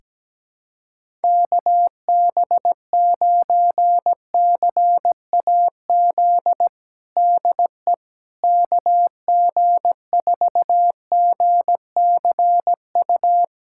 Morse code, heard as a keyed tone, about 17 wpm, 700 Hz. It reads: KB9CAZ DE KG4GCU
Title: USA Callsign DE Callsign 17wpm